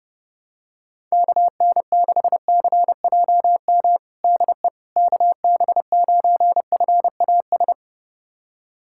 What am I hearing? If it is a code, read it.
Morse code, heard as a keyed tone, about 28 wpm, 700 Hz. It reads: KN6CJM DE KB9FAS
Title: USA Callsign DE Callsign 30wpm